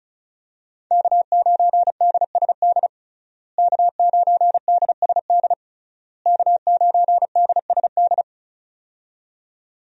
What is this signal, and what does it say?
Morse code, heard as a keyed tone, about 34 wpm, 700 Hz. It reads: K9DSD K9DSD K9DSD
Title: USA Call Signs - 5 Characters - Repeated 3x - 35wpm